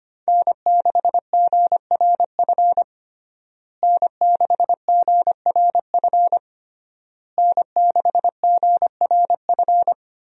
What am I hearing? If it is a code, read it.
Morse code, heard as a keyed tone, about 25 wpm, 700 Hz. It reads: N6GRF N6GRF N6GRF
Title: USA Call Signs - 5 Characters - Repeated 3x - 25wpm